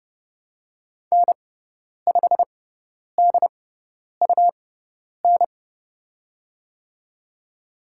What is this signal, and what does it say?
Morse code, heard as a keyed tone, about 30 wpm, spaced as for 10 wpm, 700 Hz. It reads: N5DUN
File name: Lesson 17g - Letter u - Call Signs 30wpm